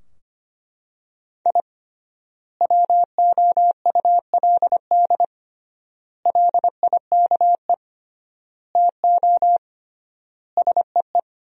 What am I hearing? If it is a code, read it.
Morse code, heard as a keyed tone, about 25 wpm, 700 Hz. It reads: I WOULD LIKE TO SEE
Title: 3x Word-Spacing - Sentences from Top 200 Words - 25wpm